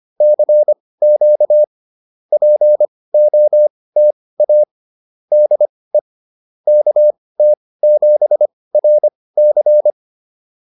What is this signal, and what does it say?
Morse code, heard as a keyed tone, about 25 wpm, spaced as for 18 wpm, 600 Hz. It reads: CQ POTA DE KT7RC
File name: POTA207 18